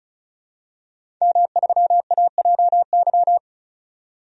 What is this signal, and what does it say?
Morse code, heard as a keyed tone, about 34 wpm, 700 Hz. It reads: M3AJY
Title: DXCC-100 Call Signs - 5 Characters - 35wpm